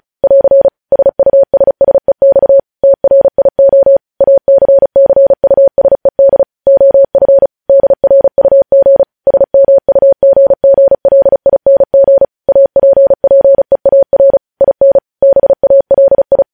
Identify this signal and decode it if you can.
Morse code, heard as a keyed tone, about 36 wpm, 550 Hz. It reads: <AR> SUSSEX TRIO ACCUSED OF DRUG SMUGGLING APPEAR IN BALI
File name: podcast / News headlines-1x-35x35